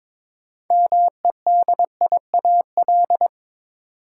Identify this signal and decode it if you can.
Morse code, heard as a keyed tone, about 22 wpm, 700 Hz. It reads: MEDIAL
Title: Practice QSO - Exchange 3 - More Info - 22wpm